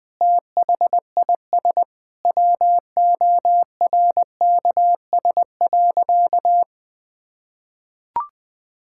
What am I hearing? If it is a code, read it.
Morse code, heard as a keyed tone, about 21 wpm, 700 Hz. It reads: THIS WORKS.
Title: Sentences with 2-Word Phrases - 20wpm